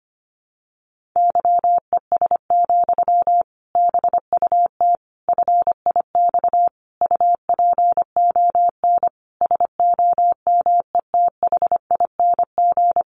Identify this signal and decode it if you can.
Morse code, heard as a keyed tone, about 25 wpm, 700 Hz. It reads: YES, BUT FIX UPON SOMETHING
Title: Sentences from Top 1400 Words 25wpm